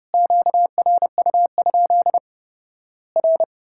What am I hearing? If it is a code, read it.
Morse code, heard as a keyed tone, about 31 wpm, 700 Hz. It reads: QRU? R
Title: Worked All Europe DX Contest - QTC Exchange - 30wpm